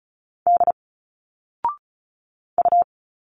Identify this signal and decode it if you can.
Morse code, heard as a keyed tone, about 36 wpm, 700 Hz. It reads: D U
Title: Single Letters 35wpm